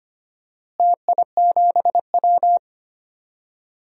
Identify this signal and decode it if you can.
Morse code, heard as a keyed tone, about 25 wpm, 700 Hz. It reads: TI7W
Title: DXCC-100 Call Signs - 4 Characters - Repeated 3x - 25wpm